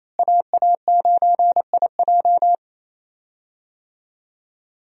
Morse code, heard as a keyed tone, about 27 wpm, 700 Hz. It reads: AA9IJ
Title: USA Call Signs - 5 Characters - Repeated 2x - 28wpm